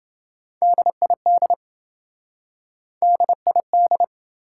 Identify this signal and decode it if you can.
Morse code, heard as a keyed tone, about 27 wpm, 700 Hz. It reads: DID DID